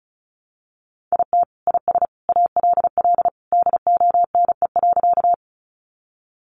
Morse code, heard as a keyed tone, about 34 wpm, 700 Hz. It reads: IT IS ALL DONE.
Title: Sentences from Top 500 Words 35wpm